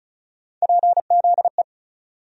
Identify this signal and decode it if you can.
Morse code, heard as a keyed tone, about 36 wpm, 700 Hz. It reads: PZE
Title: USA Call Sign Postfixes - 3 Letters 35wpm